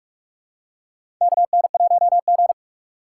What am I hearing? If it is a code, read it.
Morse code, heard as a keyed tone, about 46 wpm, 700 Hz. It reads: KN1G
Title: USA Call Signs - 4 Characters - 3x Character Spacing - 45wpm